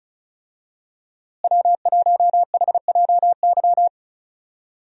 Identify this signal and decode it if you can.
Morse code, heard as a keyed tone, about 34 wpm, 700 Hz. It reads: W1HJY
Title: USA Call Signs - 5 Characters - Repeated 2x - 35wpm